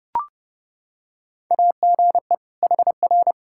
Morse code, heard as a keyed tone, about 32 wpm, 700 Hz. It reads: AGE HR
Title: K1USN SST QSO Practice 30wpm